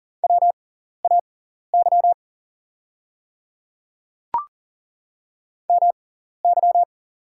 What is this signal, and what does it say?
Morse code, heard as a keyed tone, about 40 wpm, spaced as for 14 wpm, 700 Hz. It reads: WAY MY
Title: 6x Character Spacing - Top 100 Words - 40wpm